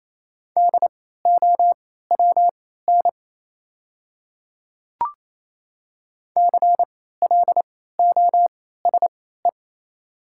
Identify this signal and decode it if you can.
Morse code, heard as a keyed tone, about 28 wpm, spaced as for 16 wpm, 690 Hz. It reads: DOWN CLOSE
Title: 3x Character Spacing - Top 500 Words - 28wpm